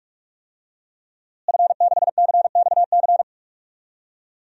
Morse code, heard as a keyed tone, about 46 wpm, 700 Hz. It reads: F6CXC
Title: DXCC-100 Call Signs - 5 Characters - Repeated 2x - 45wpm